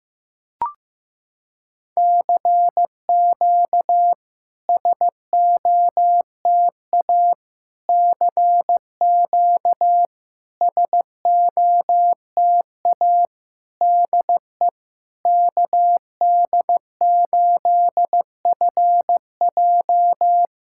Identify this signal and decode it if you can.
Morse code, heard as a keyed tone, about 16 wpm, 700 Hz. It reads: CQ SOTA CQ SOTA DE KD8FJ
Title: SOTA QSO Practice 15wpm